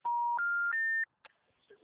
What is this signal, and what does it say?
Special information tone.